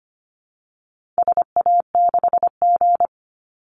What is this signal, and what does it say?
Morse code, heard as a keyed tone, about 25 wpm, 700 Hz. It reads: SA6G
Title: Call Signs Repeated 2x - 25wpm